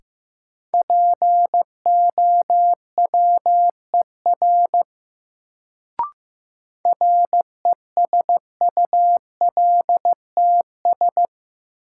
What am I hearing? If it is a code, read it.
Morse code, heard as a keyed tone, about 16 wpm, 700 Hz. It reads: POWER RESULTS
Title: Sets of 4 Words - New Word Drill - 15wpm